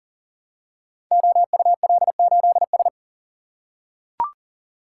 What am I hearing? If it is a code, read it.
Morse code, heard as a keyed tone, about 39 wpm, 700 Hz. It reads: OUL8S
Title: Code Groups - 5 Letter:Numbers - 40wpm